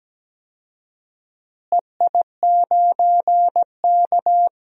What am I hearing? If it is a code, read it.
Morse code, heard as a keyed tone, about 17 wpm, 700 Hz. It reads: EI9K
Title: DXCC-100 Call Signs - 4 Characters - Repeated 2x - 17wpm